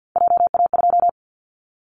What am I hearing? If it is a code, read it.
Morse code, heard as a keyed tone, about 50 wpm, 700 Hz. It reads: JA2
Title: Callsign Prefixes for Top 25 DXCC - Repeated 3x - 50wpm